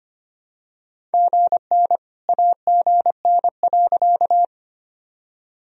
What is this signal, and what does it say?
Morse code, heard as a keyed tone, about 25 wpm, 700 Hz. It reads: GN AGN.
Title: Sentences from Top 300 Words and Abbreviations 25wpm